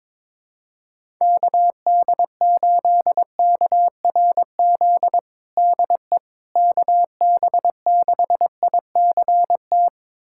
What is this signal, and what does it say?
Morse code, heard as a keyed tone, about 22 wpm, 700 Hz. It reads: KD8KRZ DE KB6ICT
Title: USA Callsign DE Callsign 22wpm